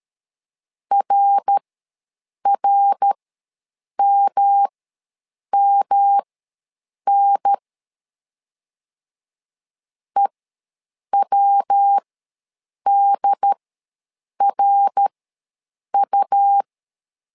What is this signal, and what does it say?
Morse code, heard as a keyed tone, about 13 wpm, spaced as for 7 wpm, 800 Hz. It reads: RRMMN EWDRU
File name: LEZIONI / 3 MWRO / 3 MWR ws 12wpm 1809